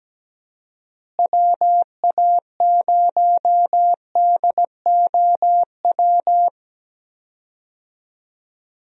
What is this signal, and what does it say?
Morse code, heard as a keyed tone, about 17 wpm, 700 Hz. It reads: WA0DOW
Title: USA Call Signs - 6 Characters - Repeated 2x - 17wpm